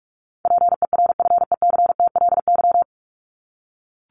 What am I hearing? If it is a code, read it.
Morse code, heard as a keyed tone, about 44 wpm, 700 Hz. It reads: PERFECTLY
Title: Top 1501-1750 Words 45wpm